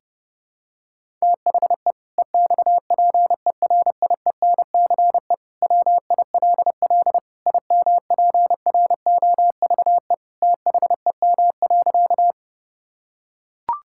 Morse code, heard as a keyed tone, about 30 wpm, 700 Hz. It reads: THE EXPERIENCE WILL IMPROVE THEM.
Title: Sentences from Top 1000 Words - Review Part 5 - 30wpm